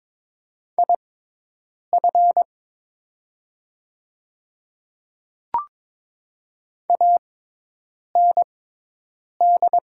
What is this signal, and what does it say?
Morse code, heard as a keyed tone, about 22 wpm, spaced as for 8 wpm, 700 Hz. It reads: IF AND
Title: 6x Character Spacing - Top 100 Words - 22wpm